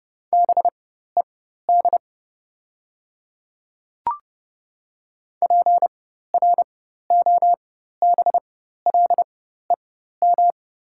Morse code, heard as a keyed tone, about 30 wpm, spaced as for 14 wpm, 690 Hz. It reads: BED PROBLEM
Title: 4x Character Spacing - Top 500 Words - 30wpm